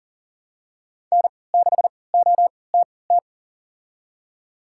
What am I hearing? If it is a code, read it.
Morse code, heard as a keyed tone, about 40 wpm, spaced as for 23 wpm, 700 Hz. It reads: N6OTT
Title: USA Call Signs - 5 Characters - 3x Character Spacing - 40wpm